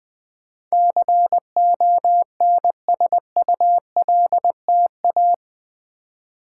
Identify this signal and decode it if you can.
Morse code, heard as a keyed tone, about 20 wpm, 700 Hz. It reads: CONSULTA
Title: Build A Word - 4 Letters At A Time - 12-Letter Words - 20wpm